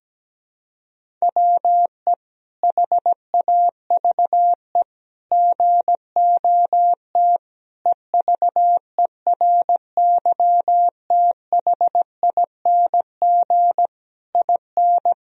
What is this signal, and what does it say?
Morse code, heard as a keyed tone, about 17 wpm, 700 Hz. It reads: WE HAVE GOT EVERYTHING IN
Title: Sentences from Top 1000 Words - Review Part 3 - 17wpm